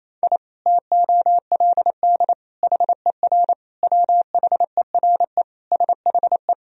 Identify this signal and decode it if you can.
Morse code, heard as a keyed tone, about 28 wpm, 700 Hz. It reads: I TOLD HER WHERE SHE
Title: Sentences from Top 1000 Words - Review Part 2 - 28wpm